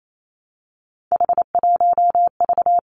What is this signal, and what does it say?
Morse code, heard as a keyed tone, about 28 wpm, 700 Hz. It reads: H1V
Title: Three Challenging Letter-Number Combinations 28wpm